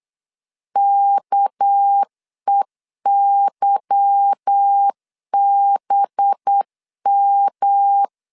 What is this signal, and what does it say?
Morse code, heard as a keyed tone, about 8 wpm, 800 Hz. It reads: KEYBM
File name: LEZIONI / 8 YJ45 / 8 Y 8wpm 3245